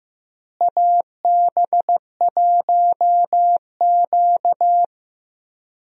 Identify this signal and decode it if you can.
Morse code, heard as a keyed tone, about 15 wpm, 700 Hz. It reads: AB1Q
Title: USA Call Signs - 4 Characters - Repeated 3x - 15wpm